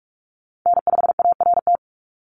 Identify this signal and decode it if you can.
Morse code, heard as a keyed tone, about 46 wpm, 690 Hz. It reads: N5ART
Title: USA Call Signs 45wpm